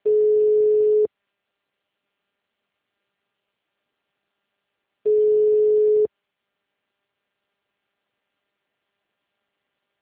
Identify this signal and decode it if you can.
Ringback tone (Europe, 425 Hz).